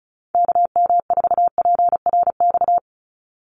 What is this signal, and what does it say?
Morse code, heard as a keyed tone, about 34 wpm, 700 Hz. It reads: KM4PRX
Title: USA Call Signs 35wpm